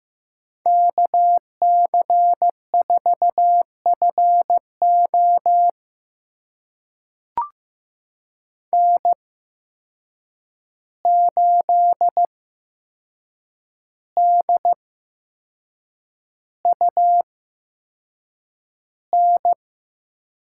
Morse code, heard as a keyed tone, about 15 wpm, spaced as for 6 wpm, 700 Hz. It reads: KC4FO N8DUN
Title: USA Call Signs - 5 Characters - 8x Character Spacing - 15wpm